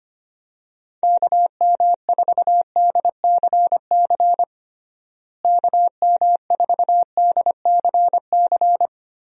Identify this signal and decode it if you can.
Morse code, heard as a keyed tone, about 25 wpm, 700 Hz. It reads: KM4DCC KM4DCC
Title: USA Call Signs - 6 Characters - Repeated 3x - 25wpm